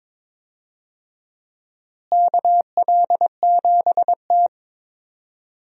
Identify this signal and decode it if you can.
Morse code, heard as a keyed tone, about 22 wpm, 700 Hz. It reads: KL7T
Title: DXCC-100 Call Signs - 4 Characters - Repeated 2x - 22wpm